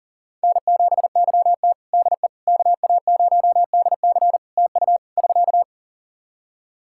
Morse code, heard as a keyed tone, about 40 wpm, 700 Hz. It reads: N7YT DE KA0DC TU <SK>
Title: Practice QSO - Exchange 4 - Ending - 40wpm